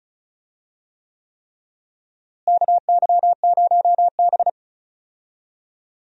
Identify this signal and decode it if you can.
Morse code, heard as a keyed tone, about 36 wpm, 700 Hz. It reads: KY0B
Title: USA Call Signs - 4 Characters - Repeated 3x - 35wpm